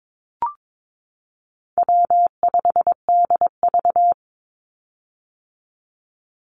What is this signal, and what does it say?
Morse code, heard as a keyed tone, about 22 wpm, 700 Hz. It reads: W5DV
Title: Call Signs - 22wpm